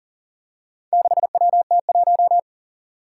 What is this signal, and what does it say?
Morse code, heard as a keyed tone, about 41 wpm, 700 Hz. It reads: 6WT1
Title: Four Challenging Letter-Number Combinations 40wpm